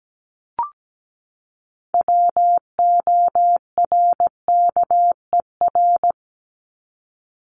Morse code, heard as a keyed tone, about 17 wpm, 700 Hz. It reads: WORKER
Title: ES:S Suffix - Top 100 Words with Base Words 17wpm